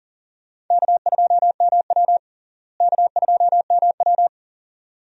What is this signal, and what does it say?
Morse code, heard as a keyed tone, about 37 wpm, 700 Hz. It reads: K2MW K2MW
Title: USA Call Signs - 4 Characters - Repeated 2x - 40wpm